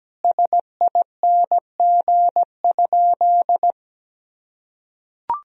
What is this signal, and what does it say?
Morse code, heard as a keyed tone, about 17 wpm, 700 Hz. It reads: SING?
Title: Sentences from Top 1000 Words - Review Part 2 - 17wpm